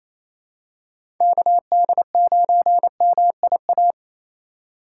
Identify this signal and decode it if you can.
Morse code, heard as a keyed tone, about 27 wpm, 700 Hz. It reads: KD9MIA
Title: USA Call Signs - 6 Characters - 3x Character Spacing - 28wpm